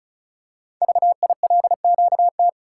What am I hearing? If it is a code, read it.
Morse code, heard as a keyed tone, about 36 wpm, 690 Hz. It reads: VILQT
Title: Code Groups - 5 Letters - 35wpm